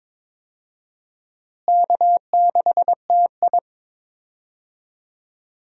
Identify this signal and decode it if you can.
Morse code, heard as a keyed tone, about 22 wpm, 700 Hz. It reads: K6TI
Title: USA Call Signs - 4 Characters - Repeated 2x - 22wpm